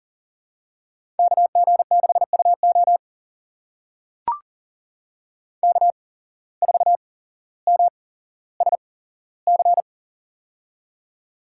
Morse code, heard as a keyed tone, about 40 wpm, spaced as for 17 wpm, 700 Hz. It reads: KG6UO K4MSC
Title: USA Call Signs - 5 Characters - 8x Character Spacing - 40wpm